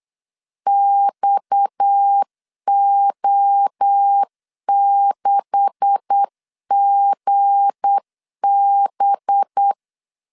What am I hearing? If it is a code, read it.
Morse code, heard as a keyed tone, about 8 wpm, 800 Hz. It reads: XO6GB
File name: LEZIONI / 9 286 8wpm 3425